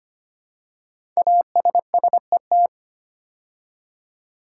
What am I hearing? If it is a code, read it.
Morse code, heard as a keyed tone, about 25 wpm, 700 Hz. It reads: ASSET
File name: 5 Letter Words - First 3 Letters Given - 25wpm